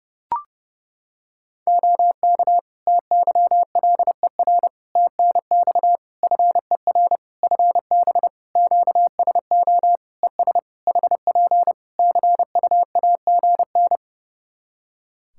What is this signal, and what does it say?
Morse code, heard as a keyed tone, about 30 wpm, 700 Hz. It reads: OK TYLER TNX FER FB QSO ES HP CUAGN
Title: Practice QSO - Exchange 4 - Ending - 30wpm